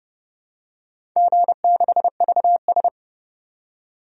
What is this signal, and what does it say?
Morse code, heard as a keyed tone, about 29 wpm, 700 Hz. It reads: G6VS
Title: DXCC-100 Call Signs - 4 Characters - 30wpm